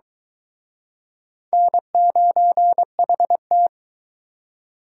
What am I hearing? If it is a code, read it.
Morse code, heard as a keyed tone, about 22 wpm, 700 Hz. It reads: N9HT